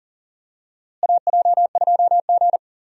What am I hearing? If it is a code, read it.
Morse code, heard as a keyed tone, about 40 wpm, 700 Hz. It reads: AJ2G
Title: USA Call Signs - 4 Characters - Repeated 2x - 40wpm